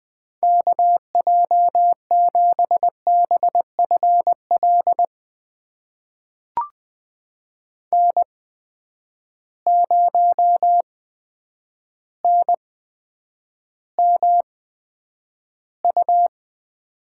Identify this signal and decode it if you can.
Morse code, heard as a keyed tone, about 20 wpm, spaced as for 9 wpm, 700 Hz. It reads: KJ7BFL N0NMU
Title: USA Call Signs - 8x Character Spacing - 20wpm